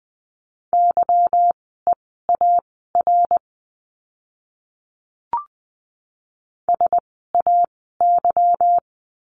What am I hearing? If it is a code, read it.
Morse code, heard as a keyed tone, about 20 wpm, spaced as for 14 wpm, 700 Hz. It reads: YEAR SAY
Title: 237 Lesson 38f - Letter q - Words 20wpm